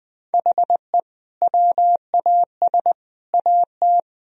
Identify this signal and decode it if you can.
Morse code, heard as a keyed tone, about 21 wpm, 700 Hz. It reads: HE WAS AT
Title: Sentences from Top 200 Words and Abbreviations 20wpm